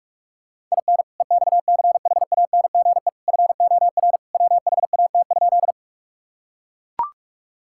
Morse code, heard as a keyed tone, about 42 wpm, 700 Hz. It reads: IN EXCHANGE FOR WHAT?
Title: Sentences from Top 1000 Words - Review Part 6 - 45wpm